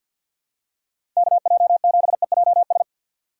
Morse code, heard as a keyed tone, about 48 wpm, 700 Hz. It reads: KJ7EJS
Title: USA Call Signs - 6 Characters - 3x Character Spacing - 50wpm